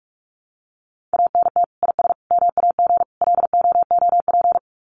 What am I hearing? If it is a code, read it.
Morse code, heard as a keyed tone, about 44 wpm, 700 Hz. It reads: ANT IS MAG LOOP
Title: Common QSO Phrases 45wpm